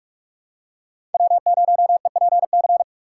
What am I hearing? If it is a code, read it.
Morse code, heard as a keyed tone, about 46 wpm, 700 Hz. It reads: W0EPC
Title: K1USN SST QSO Practice - 45wpm